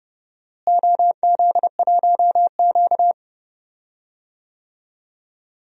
Morse code, heard as a keyed tone, about 30 wpm, 700 Hz. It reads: OZ1Q
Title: DXCC-100 Call Signs - 4 Characters - Repeated 3x - 30wpm